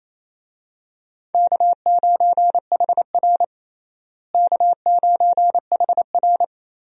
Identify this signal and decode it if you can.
Morse code, heard as a keyed tone, about 28 wpm, 700 Hz. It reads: K9HR K9HR
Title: USA Call Signs - 4 Characters - Repeated 3x - 28wpm